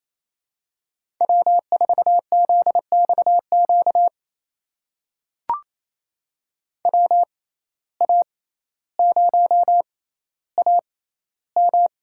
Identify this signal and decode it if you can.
Morse code, heard as a keyed tone, about 28 wpm, spaced as for 14 wpm, 700 Hz. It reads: W4ZXQ WA0AM
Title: USA Call Signs - 5 Characters - 6x Character Spacing - 28wpm